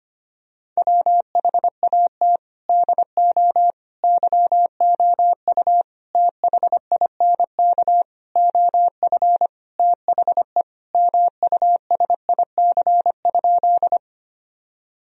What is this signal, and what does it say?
Morse code, heard as a keyed tone, about 25 wpm, 700 Hz. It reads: WHAT DO YOU THINK OF THE MUSIC?
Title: Sentences from Top 1000 Words - Review Part 1 - 25wpm